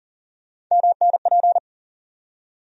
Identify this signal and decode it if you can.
Morse code, heard as a keyed tone, about 40 wpm, 700 Hz. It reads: MNP
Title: USA Call Sign Postfixes - 3 Letters 40wpm